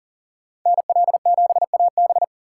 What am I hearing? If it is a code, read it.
Morse code, heard as a keyed tone, about 40 wpm, 700 Hz. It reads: NL7AB
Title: USA Call Signs - 5 Characters - 2x Character Spacing - 40wpm